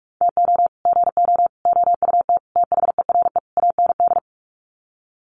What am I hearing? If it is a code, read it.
Morse code, heard as a keyed tone, about 44 wpm, 690 Hz. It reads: TO GO OUT THERE AND
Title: Sets of 5 Words Encore2 45wpm